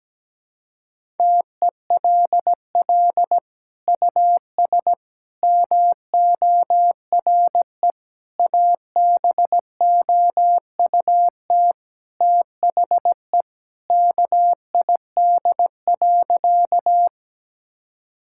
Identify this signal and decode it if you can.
Morse code, heard as a keyed tone, about 17 wpm, 700 Hz. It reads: TELL US MORE ABOUT THE KID.
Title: Sentences from Top 1600 Words 17wpm